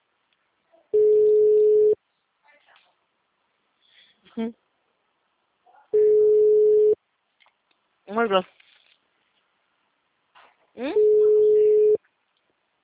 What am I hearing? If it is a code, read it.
Ringback tone (Europe, 425 Hz), with speech.